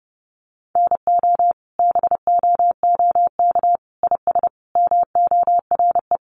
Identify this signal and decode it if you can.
Morse code, heard as a keyed tone, about 30 wpm, 700 Hz. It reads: NO BOOK IS MORE
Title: Sentences from Top 1200 Words 30wpm